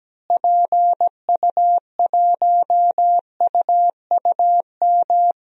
Morse code, heard as a keyed tone, about 17 wpm, 700 Hz. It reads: PU1UUM
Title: DXCC-100 Call Signs - 6 Characters - Repeated 2x - 17wpm